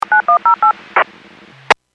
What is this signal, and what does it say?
DTMF, heard as telephone touch tones, tones of about 120 ms. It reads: 9108
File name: Businesses / 00404 02-27 Businesses 451.25 WPS